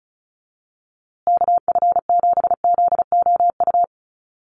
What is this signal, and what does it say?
Morse code, heard as a keyed tone, about 34 wpm, 700 Hz. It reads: KF7ZOU
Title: USA Call Signs Repeated 3x 35wpm